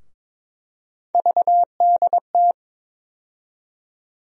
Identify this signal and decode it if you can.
Morse code, heard as a keyed tone, about 22 wpm, 700 Hz. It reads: VDT
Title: USA Call Sign Postfixes - 3 Letters 22wpm